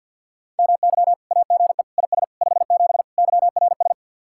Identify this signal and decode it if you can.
Morse code, heard as a keyed tone, about 60 wpm, 700 Hz. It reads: MY AGE IS 57 YRS
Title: Speed-Racing Common QSO Phrases Encore - 50wpm